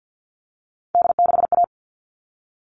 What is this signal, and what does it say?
Morse code, heard as a keyed tone, about 50 wpm, 700 Hz. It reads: D6A
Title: All Cut Numbers - 3-Digit 50wpm